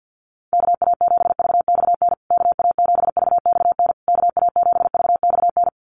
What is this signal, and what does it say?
Morse code, heard as a keyed tone, about 50 wpm, 700 Hz. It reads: KA7VXN KA7VXN KA7VXN
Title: USA Call Signs Repeated 3x 50wpm